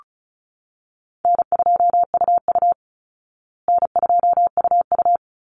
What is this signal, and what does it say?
Morse code, heard as a keyed tone, about 34 wpm, 700 Hz. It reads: N2UU N2UU
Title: Call Signs Repeated 3x - 35wpm